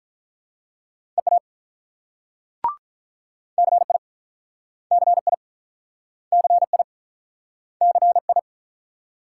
Morse code, heard as a keyed tone, about 47 wpm, 700 Hz. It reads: EA CI CI CI CI